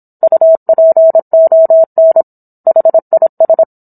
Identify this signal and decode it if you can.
Morse code, heard as a keyed tone, about 27 wpm, 650 Hz. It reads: UPON HIS
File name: Book One / 17-Chapter 17 - 26 WPM